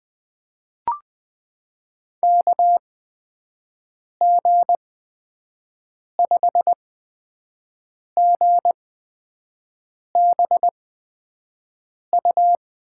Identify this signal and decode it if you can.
Morse code, heard as a keyed tone, about 21 wpm, spaced as for 5 wpm, 700 Hz. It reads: KG5GBU
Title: USA Call Signs - 8x Character Spacing - 20wpm